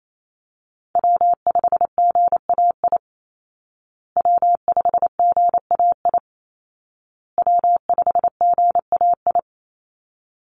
Morse code, heard as a keyed tone, about 28 wpm, 700 Hz. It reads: W5GAI W5GAI W5GAI
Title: Call Signs Repeated 2x - 28wpm